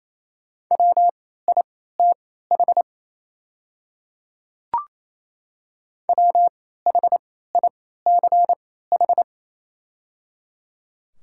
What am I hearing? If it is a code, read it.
Morse code, heard as a keyed tone, about 28 wpm, spaced as for 16 wpm, 690 Hz. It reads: WITH WHICH
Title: 3x Character Spacing - Top 100 Words - 28wpm